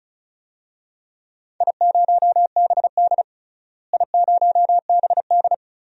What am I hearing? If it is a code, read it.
Morse code, heard as a keyed tone, about 36 wpm, 700 Hz. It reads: I0BD I0BD
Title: DXCC-100 Call Signs - 4 Characters - Repeated 2x - 35wpm